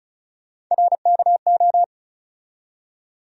Morse code, heard as a keyed tone, about 36 wpm, 700 Hz. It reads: RKO
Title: USA Call Sign Postfixes - 3 Letters 35wpm